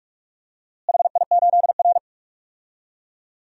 Morse code, heard as a keyed tone, about 46 wpm, 700 Hz. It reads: HI8R
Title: DXCC-100 Call Signs - 4 Characters - Repeated 2x - 45wpm